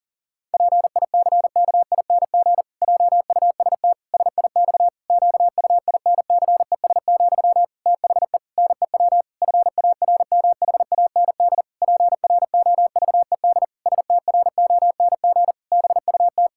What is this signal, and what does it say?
Morse code, heard as a keyed tone, about 37 wpm, 700 Hz. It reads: PICKING JUST SIX QUINCES, THE NEW FARMHAND PROVED STRONG BUT
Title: Pangram Challenge v2 - 40wpm